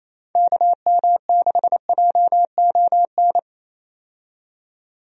Morse code, heard as a keyed tone, about 27 wpm, 700 Hz. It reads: KM6JON
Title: USA Call Signs - 6 Characters - 28wpm